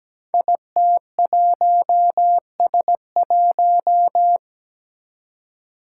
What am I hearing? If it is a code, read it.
Morse code, heard as a keyed tone, about 17 wpm, 700 Hz. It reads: IT1S1
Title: Five Challenging Letter-Number Combinations 17wpm